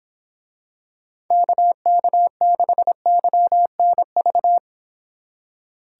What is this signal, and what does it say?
Morse code, heard as a keyed tone, about 26 wpm, 700 Hz. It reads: KK6YNV